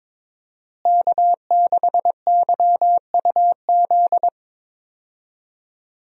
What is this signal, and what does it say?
Morse code, heard as a keyed tone, about 22 wpm, 700 Hz. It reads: K6YUZ
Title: USA Call Signs - 5 Characters - Repeated 2x - 22wpm